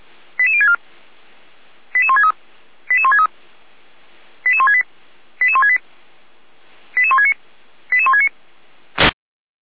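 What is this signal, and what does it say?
Five-tone selective call (ZVEI), heard as quick runs of five tones, tones of about 70 ms. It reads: 88064 88163 88163 88168 88168 88169 88169
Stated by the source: Received on an SDR.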